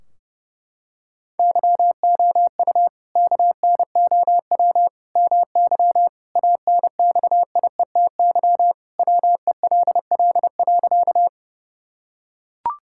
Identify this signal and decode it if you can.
Morse code, heard as a keyed tone, about 30 wpm, 700 Hz. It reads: YOU KNOW MY ANXIETY WELL.
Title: Sentences from Top 1600 Words 30wpm